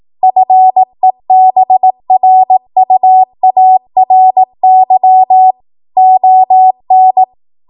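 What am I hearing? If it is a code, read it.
Morse code, heard as a keyed tone, about 18 wpm, 750 Hz. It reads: FEBRUARY ON
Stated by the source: Code practice at 18 WPM, a transition run bridging 15 to 20 WPM.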